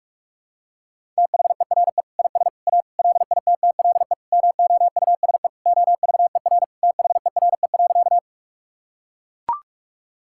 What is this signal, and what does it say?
Morse code, heard as a keyed tone, about 42 wpm, 700 Hz. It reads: THERE IS A LITTLE MOUSE OVER THERE.
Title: Sentences from Top 1000 Words - Review Part 5 - 45wpm